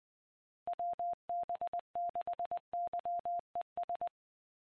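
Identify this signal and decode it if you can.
Morse code, heard as a keyed tone, about 20 wpm, 700 Hz. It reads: WB6YES
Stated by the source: Morse ident of the PAPA System analog repeaters, WD6FZA.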